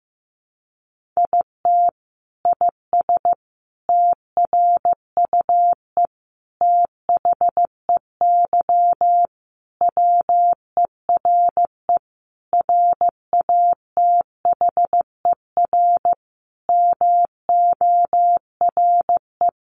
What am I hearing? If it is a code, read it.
Morse code, heard as a keyed tone, about 15 wpm, 700 Hz. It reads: IT IS TRUE THEY WERE RATHER MORE
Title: Sentences from Top 1400 Words 15wpm